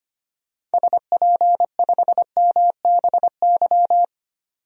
Morse code, heard as a keyed tone, about 25 wpm, 700 Hz. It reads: SP5MBY
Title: DXCC-100 Call Signs - 6 Characters - 25wpm